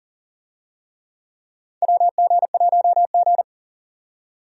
Morse code, heard as a keyed tone, about 39 wpm, 700 Hz. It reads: WG1G
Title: USA Call Signs - 4 Characters - 40wpm